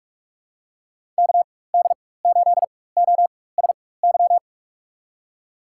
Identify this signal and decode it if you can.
Morse code, heard as a keyed tone, about 42 wpm, spaced as for 21 wpm, 700 Hz. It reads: KD8OSY
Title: USA Call Signs - 6 Characters - 4x Character Spacing - 45wpm